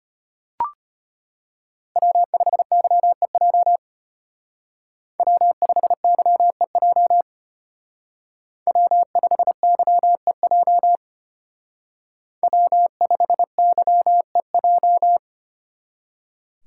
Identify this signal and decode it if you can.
Morse code, heard as a keyed tone, about 32 wpm, 700 Hz. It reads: W5YEJ W5YEJ W5YEJ W5YEJ